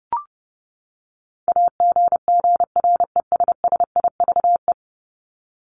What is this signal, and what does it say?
Morse code, heard as a keyed tone, about 30 wpm, 700 Hz. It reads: AGGRESSIVE
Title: Top 1501-1750 Words 30wpm